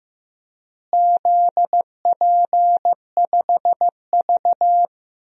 Morse code, heard as a keyed tone, about 15 wpm, 700 Hz. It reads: ZP5V
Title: DXCC-100 Call Signs - 4 Characters - Repeated 3x - 15wpm